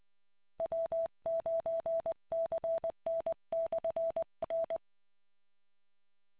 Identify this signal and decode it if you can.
Morse code, heard as a keyed tone, about 20 wpm, 660 Hz. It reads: W9CN/N